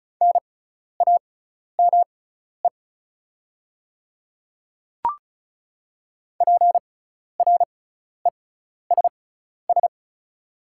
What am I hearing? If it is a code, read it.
Morse code, heard as a keyed tone, about 36 wpm, spaced as for 12 wpm, 700 Hz. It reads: NAME PRESS
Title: 6x Character Spacing - Top 500 Words - 35wpm